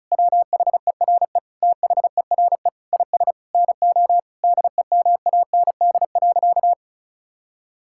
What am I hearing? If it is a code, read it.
Morse code, heard as a keyed tone, about 34 wpm, 700 Hz. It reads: WHERE THERE IS NO DEMAND.
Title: Sentences from Top 1000 Words - Review Part 5 - 35wpm